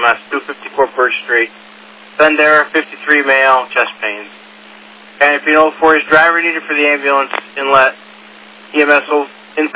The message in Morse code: none

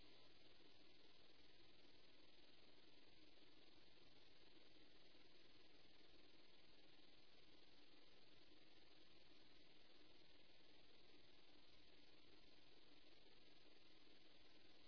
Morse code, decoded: none

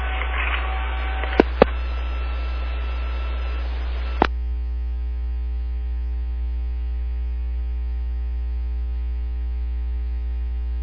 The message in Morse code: none

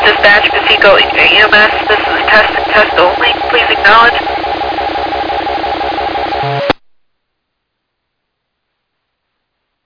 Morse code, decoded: none

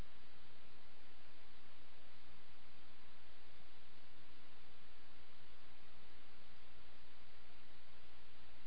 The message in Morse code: none

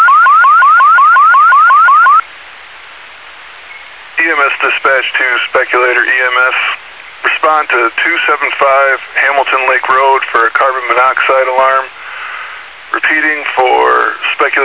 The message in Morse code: none

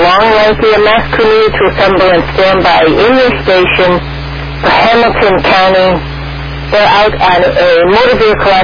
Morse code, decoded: none

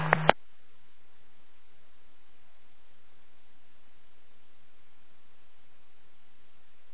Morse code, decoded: none

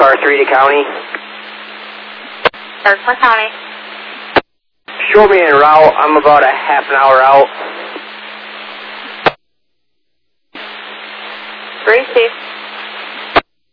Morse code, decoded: none

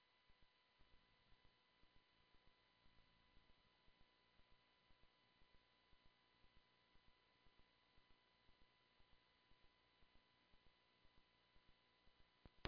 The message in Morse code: none

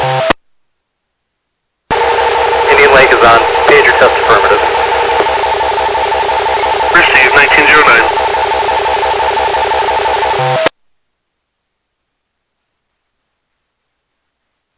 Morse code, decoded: none

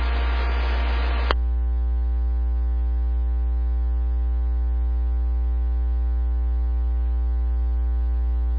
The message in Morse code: none

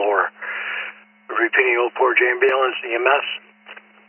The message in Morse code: none

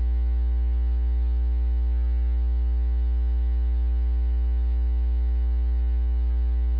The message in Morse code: none

